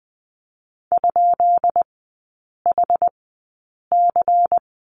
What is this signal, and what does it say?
Morse code, heard as a keyed tone, about 20 wpm, spaced as for 8 wpm, 700 Hz. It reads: ?HC
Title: 152 Lesson 25e - Letter p - Three Characters 20wpm